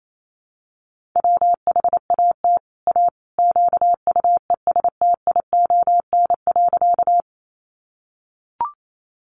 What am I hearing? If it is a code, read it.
Morse code, heard as a keyed tone, about 28 wpm, 700 Hz. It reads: WHAT A QUESTION.
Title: Sentences from Top 400 Words 28wpm